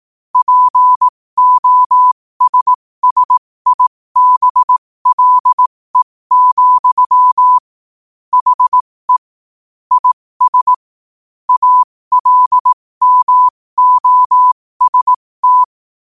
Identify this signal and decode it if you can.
Morse code, heard as a keyed tone, about 18 wpm, spaced as for 15 wpm, 1000 Hz. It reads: POSSIBLE, HE IS ALMOST
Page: New quotes every day in morse code at 15 Words per minute.